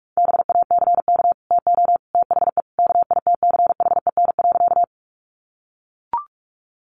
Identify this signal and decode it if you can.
Morse code, heard as a keyed tone, about 44 wpm, 700 Hz. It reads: BACK TO THE KITCHEN.
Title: Sentences from Top 1400 Words 45wpm